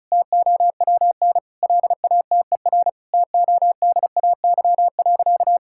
Morse code, heard as a keyed tone, about 34 wpm, 700 Hz. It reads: TOWN LATER TODAY.
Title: Sentences from Top 1000 Words - Review Part 6 - 35wpm